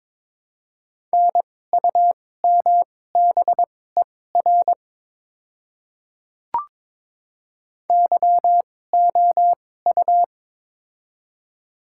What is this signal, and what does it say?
Morse code, heard as a keyed tone, about 22 wpm, spaced as for 16 wpm, 700 Hz. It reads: NUMBER YOU
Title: 2x Character Spacing - Top 100 Words - 22wpm